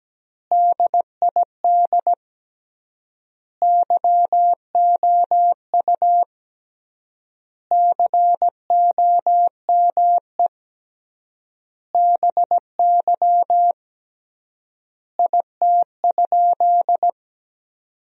Morse code, heard as a keyed tone, about 17 wpm, 700 Hz. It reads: DID YOU COME BY IT?
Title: 3x Word-Spacing - Sentences from Top 100 Words - 17wpm